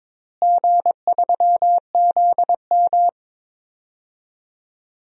Morse code, heard as a keyed tone, about 22 wpm, 700 Hz. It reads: G3ZM
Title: DXCC-100 Call Signs - 4 Characters - Repeated 3x - 22wpm